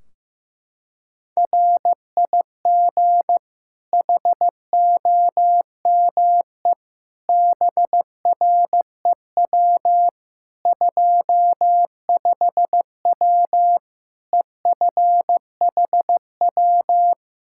Morse code, heard as a keyed tone, about 15 wpm, 700 Hz. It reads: RIG HOME BREW 25W EFHW
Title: K1USN SST QSO Practice 15wpm